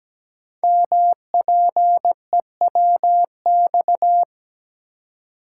Morse code, heard as a keyed tone, about 17 wpm, 700 Hz. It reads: MPEWX
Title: Code Groups - 5 Letters - 17wpm